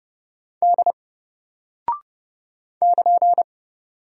Morse code, heard as a keyed tone, about 30 wpm, 700 Hz. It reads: D <KN>
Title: Single Character (Letter:Number:Punctuation:Prosign) 30wpm